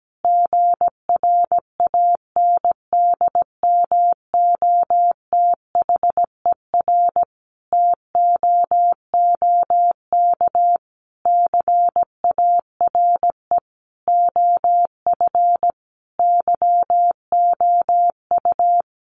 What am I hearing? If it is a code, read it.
Morse code, heard as a keyed tone, about 17 wpm, 700 Hz. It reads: GRANDMOTHER TOOK CARE OF YOU
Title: Sentences from Top 1200 Words 17wpm